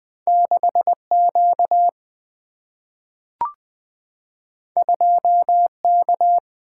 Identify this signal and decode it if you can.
Morse code, heard as a keyed tone, about 20 wpm, 700 Hz. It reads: 6Q 2K
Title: Single-Letter Single-Number 20wpm